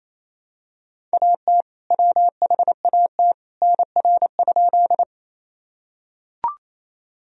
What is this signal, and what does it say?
Morse code, heard as a keyed tone, about 28 wpm, 700 Hz. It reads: AT WHAT NR?
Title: Sentences from Top 100 Words and Abbreviations 28wpm